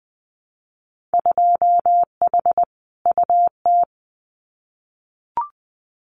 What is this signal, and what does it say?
Morse code, heard as a keyed tone, about 20 wpm, 700 Hz. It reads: 2H UT
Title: Winter Field Day Exchanges Encore 20wpm